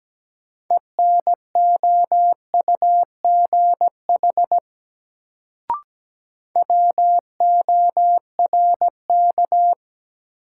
Morse code, heard as a keyed tone, about 17 wpm, 700 Hz. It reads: ENOUGH WORK
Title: Sets of 2 Words - New Word Drill - 17wpm